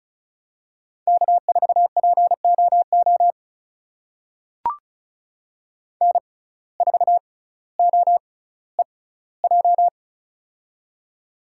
Morse code, heard as a keyed tone, about 36 wpm, spaced as for 18 wpm, 700 Hz. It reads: K4POO N4OEJ
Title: USA Call Signs - 5 Characters - 6x Character Spacing - 35wpm